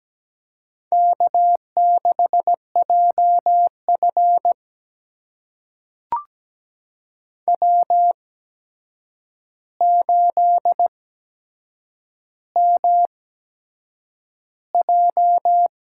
Morse code, heard as a keyed tone, about 17 wpm, spaced as for 7 wpm, 700 Hz. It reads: K6JF W8MJ
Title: USA Call Signs - 4 Characters - 8x Character Spacing - 17wpm